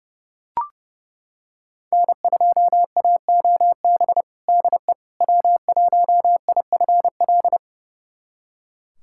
Morse code, heard as a keyed tone, about 32 wpm, 700 Hz. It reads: N2AOB DE W1IFL
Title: USA Callsign DE Callsign - 5 Characters - 30wpm